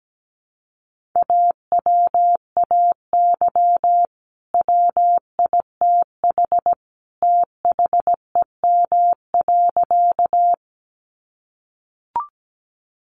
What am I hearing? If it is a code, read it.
Morse code, heard as a keyed tone, about 17 wpm, 700 Hz. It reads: AWAY WITH THEM.
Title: Sentences from Top 600 Words 17wpm